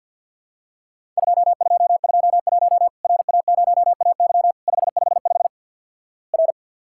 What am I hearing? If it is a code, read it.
Morse code, heard as a keyed tone, about 50 wpm, 700 Hz. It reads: 2221 RA0AY 555 R
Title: Worked All Europe DX Contest - QTC Exchange with Repeats - 50wpm